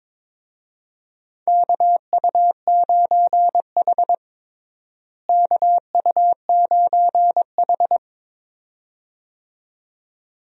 Morse code, heard as a keyed tone, about 22 wpm, 700 Hz. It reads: KU9H KU9H
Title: USA Call Signs - 4 Characters - Repeated 2x - 22wpm